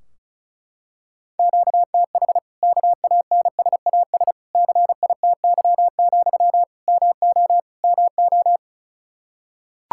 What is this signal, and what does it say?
Morse code, heard as a keyed tone, about 36 wpm, 700 Hz. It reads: QTH KANSAS CITY, MO MO
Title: Practice QSO - Exchange 1 - Beginning - 35wpm